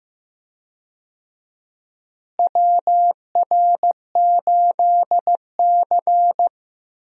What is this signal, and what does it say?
Morse code, heard as a keyed tone, about 15 wpm, 700 Hz. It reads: WR8C
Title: USA Call Signs - 4 Characters - Repeated 2x - 15wpm